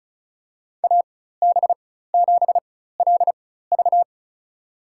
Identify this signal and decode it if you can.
Morse code, heard as a keyed tone, about 36 wpm, spaced as for 16 wpm, 700 Hz. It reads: AB7LV
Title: USA Call Signs - 4x Character Spacing - 35wpm